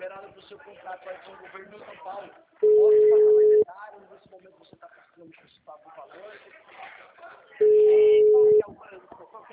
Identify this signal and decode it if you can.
Ringback tone (Europe, 425 Hz), with speech.